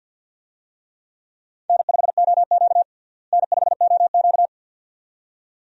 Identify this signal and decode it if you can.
Morse code, heard as a keyed tone, about 52 wpm, 700 Hz. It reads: N5OQ N5OQ
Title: USA Call Signs - 4 Characters - Repeated 2x - 50wpm